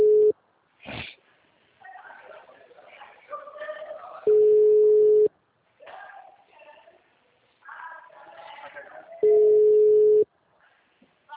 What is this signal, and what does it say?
Ringback tone (Europe, 425 Hz), with speech.
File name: out- -2016- - - .67391